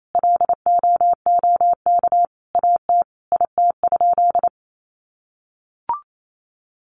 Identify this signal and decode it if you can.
Morse code, heard as a keyed tone, about 28 wpm, 700 Hz. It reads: LOOK AT IT?
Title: Sentences from Top 100 Words 28wpm